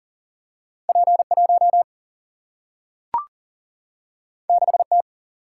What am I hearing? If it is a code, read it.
Morse code, heard as a keyed tone, about 40 wpm, 700 Hz. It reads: P 1 6 T
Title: Single-Letter Single-Number 40wpm